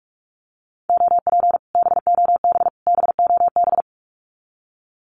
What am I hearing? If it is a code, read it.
Morse code, heard as a keyed tone, about 46 wpm, 700 Hz. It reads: OP BOB BOB
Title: Common QSO Phrases Encore 45wpm